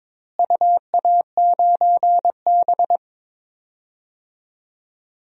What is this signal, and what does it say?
Morse code, heard as a keyed tone, about 22 wpm, 700 Hz. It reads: UA9B
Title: DXCC-100 Call Signs - 4 Characters - Repeated 2x - 22wpm